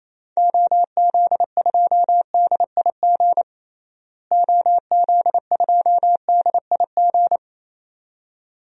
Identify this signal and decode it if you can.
Morse code, heard as a keyed tone, about 28 wpm, 700 Hz. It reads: OZ2DIG OZ2DIG
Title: DXCC-100 Call Signs - 6 Characters - Repeated 2x - 28wpm